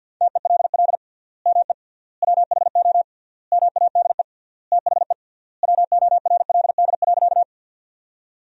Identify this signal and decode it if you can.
Morse code, heard as a keyed tone, about 50 wpm, 700 Hz. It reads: TELL ME WHO MADE THE WORLD.
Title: 3x Word-Spacing - Sentences from Top 200 Words - 50wpm